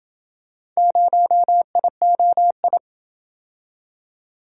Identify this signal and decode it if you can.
Morse code, heard as a keyed tone, about 28 wpm, 700 Hz. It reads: 0IOI